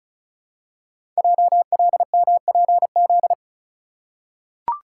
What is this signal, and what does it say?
Morse code, heard as a keyed tone, about 36 wpm, 700 Hz. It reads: JLMPZ
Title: Code Groups - 5 Letters - 35wpm